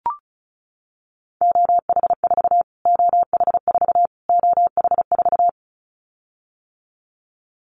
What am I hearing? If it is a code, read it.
Morse code, heard as a keyed tone, about 34 wpm, 700 Hz. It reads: OH4 OH4 OH4
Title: Callsign Prefixes for Top 25 DXCC - Repeated 3x - 35wpm